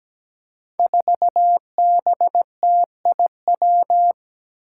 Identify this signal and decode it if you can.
Morse code, heard as a keyed tone, about 17 wpm, 700 Hz. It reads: 4BTIW
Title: Five Challenging Letter-Number Combinations 17wpm